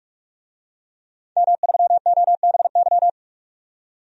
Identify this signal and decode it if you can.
Morse code, heard as a keyed tone, about 48 wpm, 700 Hz. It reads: M3OBY
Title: DXCC-100 Call Signs - 5 Characters - 45wpm